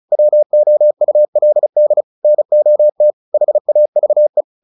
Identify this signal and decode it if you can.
Morse code, heard as a keyed tone, about 34 wpm, 600 Hz. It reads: WOULD NOT HAVE
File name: the old man and the sea / the old man and the sea 1 35wpm